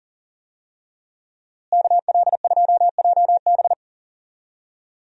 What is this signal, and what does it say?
Morse code, heard as a keyed tone, about 40 wpm, 700 Hz. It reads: KL2JB
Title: USA Call Signs - 5 Characters - Repeated 3x - 40wpm